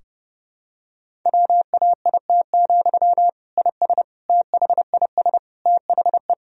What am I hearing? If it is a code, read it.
Morse code, heard as a keyed tone, about 30 wpm, 700 Hz. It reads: WAIT, IS THIS THE
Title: Sentences from Top 1000 Words - Review Part 4 - 30wpm